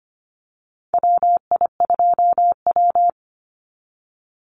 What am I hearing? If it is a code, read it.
Morse code, heard as a keyed tone, about 25 wpm, 700 Hz. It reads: WI2W
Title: NA SOTA Chaser Callsigns 25wpm